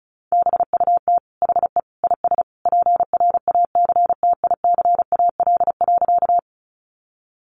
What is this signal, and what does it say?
Morse code, heard as a keyed tone, about 34 wpm, 700 Hz. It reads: BUT HE IS PRACTICAL.
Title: Sentences from Top 1400 Words 35wpm